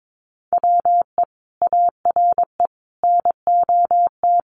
Morse code, heard as a keyed tone, about 22 wpm, 700 Hz. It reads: WE ARE NOT
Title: Sentences from Top 1200 Words 22wpm